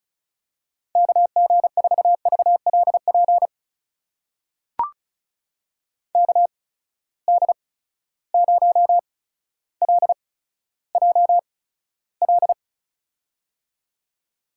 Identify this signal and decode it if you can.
Morse code, heard as a keyed tone, about 36 wpm, spaced as for 15 wpm, 700 Hz. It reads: KG4VLP KD0LJL
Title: USA Call Signs - 6 Characters - 8x Character Spacing - 35wpm